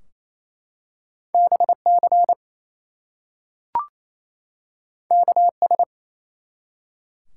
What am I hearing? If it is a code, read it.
Morse code, heard as a keyed tone, about 29 wpm, 700 Hz. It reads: BC KS
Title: US States and Canadian Provinces 28wpm